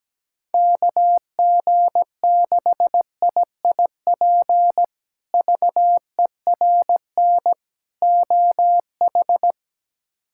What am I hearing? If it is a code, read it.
Morse code, heard as a keyed tone, about 17 wpm, 700 Hz. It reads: KG6IIP VERN OH
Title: K1USN SST QSO Practice - 17wpm